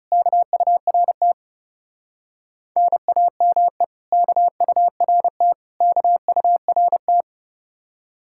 Morse code, heard as a keyed tone, about 40 wpm, 700 Hz. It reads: KURT NAME KURT KURT